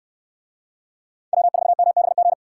Morse code, heard as a keyed tone, about 64 wpm, 700 Hz. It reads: W4MBG